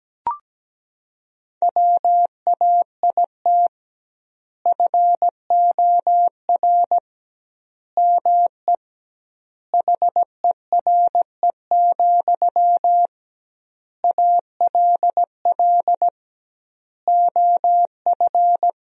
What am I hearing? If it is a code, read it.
Morse code, heard as a keyed tone, about 17 wpm, 700 Hz. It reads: WAIT FOR ME HERE, ALL OF
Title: 2x Word-Spacing - Sentences from Top 500 Words - 17wpm